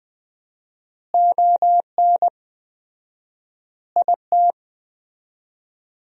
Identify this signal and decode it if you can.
Morse code, heard as a keyed tone, about 21 wpm, 700 Hz. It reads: ON IT
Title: 4x Word-Spacing - Sentences from Top 400 Words - 20wpm